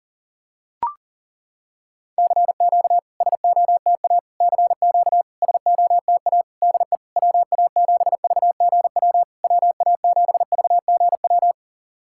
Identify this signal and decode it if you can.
Morse code, heard as a keyed tone, about 40 wpm, 690 Hz. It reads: CQ SOTA CQ SOTA DE WA7VGW WA7VGW
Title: SOTA QSO Practice 40wpm